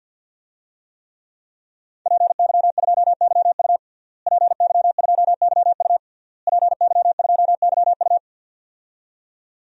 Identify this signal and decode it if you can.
Morse code, heard as a keyed tone, about 52 wpm, 700 Hz. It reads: PY2YU PY2YU PY2YU
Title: DXCC-100 Call Signs - 5 Characters - Repeated 3x - 50wpm